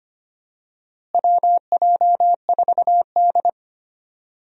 Morse code, heard as a keyed tone, about 25 wpm, 700 Hz. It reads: WJ4D
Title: USA Call Signs - 4 Characters - Repeated 3x - 25wpm